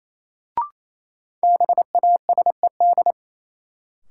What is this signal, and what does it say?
Morse code, heard as a keyed tone, about 28 wpm, 700 Hz. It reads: BASED
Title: Sets of 2 Words - New Word Drill - 28wpm